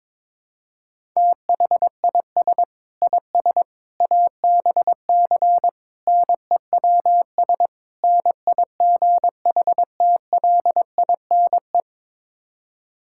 Morse code, heard as a keyed tone, about 22 wpm, 700 Hz. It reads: THIS IS ABC NEWS NIGHTLINE
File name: practice / 22wpm / Build A Phrase - 5-Word Phrases - 22wpm